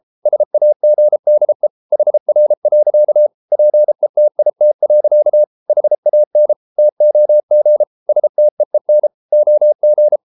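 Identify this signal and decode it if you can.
Morse code, heard as a keyed tone, about 33 wpm, 600 Hz. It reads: SAGDE HR. PETIT. HAN TOG STEEN OG
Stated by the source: Morse code at 33 wpm, QRQ.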